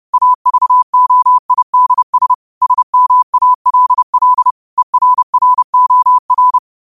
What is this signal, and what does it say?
Morse code, heard as a keyed tone, about 30 wpm, 1000 Hz. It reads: AVOIDS SMALL ERROR
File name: qotdfiles / 30wpm